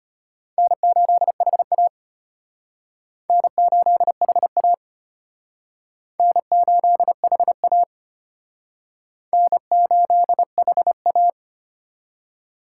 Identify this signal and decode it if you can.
Morse code, heard as a keyed tone, about 32 wpm, 700 Hz. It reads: N8HA N8HA N8HA N8HA